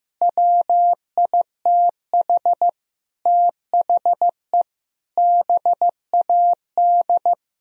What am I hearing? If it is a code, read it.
Morse code, heard as a keyed tone, about 15 wpm, 700 Hz. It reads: WITH THE BAD
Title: Sentences from Top 1000 Words - Review Part 3 - 15wpm